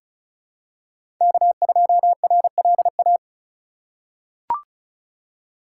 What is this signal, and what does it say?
Morse code, heard as a keyed tone, about 36 wpm, 700 Hz. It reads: K2RLA
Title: Code Groups - 5 Letter:Numbers - 35wpm